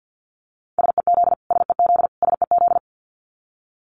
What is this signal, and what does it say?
Morse code, heard as a keyed tone, about 50 wpm, 700 Hz. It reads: SEZ SEZ SEZ
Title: Abbreviations Repeated 3x - 50wpm